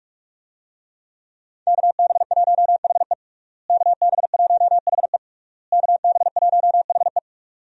Morse code, heard as a keyed tone, about 41 wpm, 700 Hz. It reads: KB1HE KB1HE KB1HE
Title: USA Call Signs - 5 Characters - Repeated 3x - 45wpm